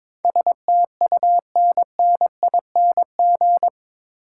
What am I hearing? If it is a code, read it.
Morse code, heard as a keyed tone, about 22 wpm, 700 Hz. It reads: STUNNING
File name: practice / 22wpm / Eight Letter Words - 22wpm